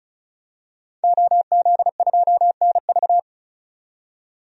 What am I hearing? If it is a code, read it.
Morse code, heard as a keyed tone, about 34 wpm, 700 Hz. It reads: OZ2NV
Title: DXCC-100 Call Signs - 5 Characters - 35wpm